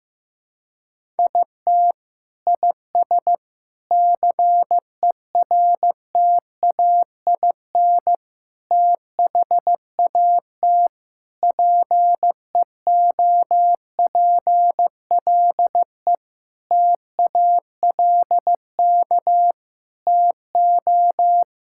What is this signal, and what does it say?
Morse code, heard as a keyed tone, about 16 wpm, 700 Hz. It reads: IT IS CERTAIN THAT PEOPLE TALK TO
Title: Sentences from Top 1000 Words - Review Part 5 - 15wpm